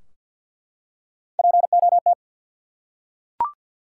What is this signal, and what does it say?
Morse code, heard as a keyed tone, about 48 wpm, 700 Hz. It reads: POT
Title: Three Letter Words - 50wpm